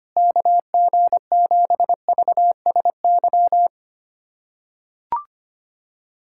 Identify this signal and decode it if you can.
Morse code, heard as a keyed tone, about 25 wpm, 700 Hz. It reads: KG7VSY
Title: USA Callsign DE Callsign 25wpm